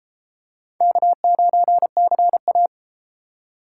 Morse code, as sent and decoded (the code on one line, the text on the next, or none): -.- ----. -.-. .-
K9CA